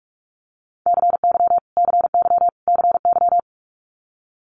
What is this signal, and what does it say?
Morse code, heard as a keyed tone, about 46 wpm, 700 Hz. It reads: CY CY CY
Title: Multi-letter Phonemes Repeated 3x 45wpm